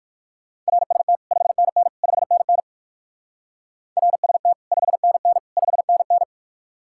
Morse code, heard as a keyed tone, about 58 wpm, 700 Hz. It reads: RST 5NN 5NN RST 5NN 5NN